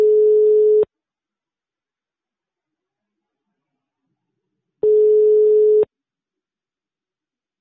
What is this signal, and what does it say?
Ringback tone (Europe, 425 Hz).
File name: sauvegarde / ACRRecordings- -jusquau 17-juin-2017 / Km€ -30€ Patrik Taxizara 2 2017 06 17 19 03 03